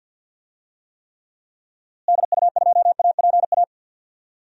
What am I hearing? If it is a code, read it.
Morse code, heard as a keyed tone, about 51 wpm, 700 Hz. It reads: DU2APA
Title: DXCC-100 Call Signs - 6 Characters - Repeated 3x - 50wpm